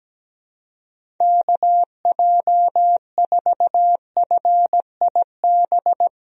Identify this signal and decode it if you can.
Morse code, heard as a keyed tone, about 17 wpm, 700 Hz. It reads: KJ4FIB
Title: USA Call Signs - 6 Characters - Repeated 3x - 17wpm